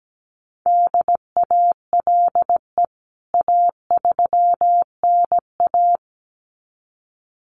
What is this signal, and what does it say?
Morse code, heard as a keyed tone, about 17 wpm, 700 Hz. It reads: DALE A3NA
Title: CWT Exchange 17wpm